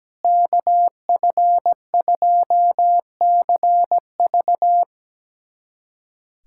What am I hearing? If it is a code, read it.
Morse code, heard as a keyed tone, about 17 wpm, 700 Hz. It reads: KF2CV
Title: USA Callsign DE Callsign - 5 Characters - 17wpm